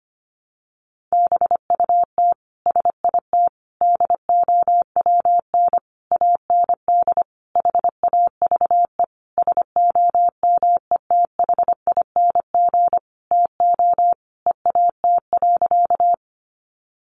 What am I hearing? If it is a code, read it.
Morse code, heard as a keyed tone, about 25 wpm, 700 Hz. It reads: BUT SIT DOWN AND HAVE SOMETHING TO EAT.
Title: Sentences from Top 600 Words 25wpm